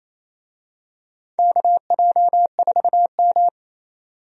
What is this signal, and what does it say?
Morse code, heard as a keyed tone, about 27 wpm, 700 Hz. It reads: KJ4M
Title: USA Call Signs - 4 Characters - 28wpm